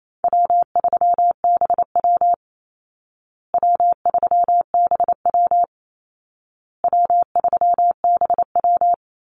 Morse code, heard as a keyed tone, about 28 wpm, 700 Hz. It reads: W3BW W3BW W3BW
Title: Easy Call Signs Repeated 3x - 28wpm